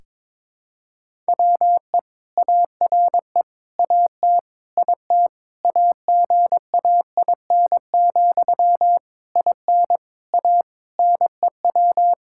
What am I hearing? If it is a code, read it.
Morse code, heard as a keyed tone, about 22 wpm, 700 Hz. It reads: WE ARE AT IT AGAIN, IN A NEW
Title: Sentences from Top 1000 Words - Review Part 5 - 22wpm